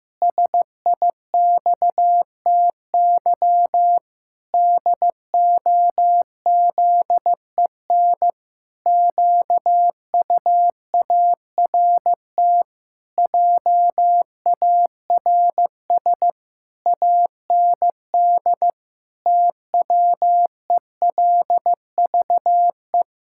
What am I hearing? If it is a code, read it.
Morse code, heard as a keyed tone, about 15 wpm, 700 Hz. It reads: SIXTY DOZEN QUART JARS AND TWELVE
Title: Pangram Challenge v2 - 15wpm